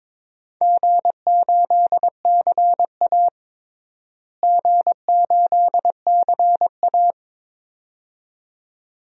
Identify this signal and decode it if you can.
Morse code, heard as a keyed tone, about 22 wpm, 700 Hz. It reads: G8CA G8CA
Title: DXCC-100 Call Signs - 4 Characters - Repeated 2x - 22wpm